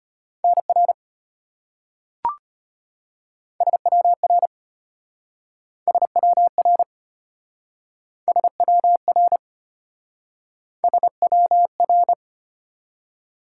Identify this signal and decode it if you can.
Morse code, heard as a keyed tone, about 34 wpm, 700 Hz. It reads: NR SWR SWR SWR SWR